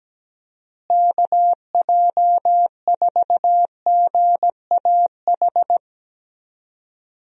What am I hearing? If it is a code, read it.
Morse code, heard as a keyed tone, about 17 wpm, 700 Hz. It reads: KJ4GAH
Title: USA Call Signs - 6 Characters - Repeated 3x - 17wpm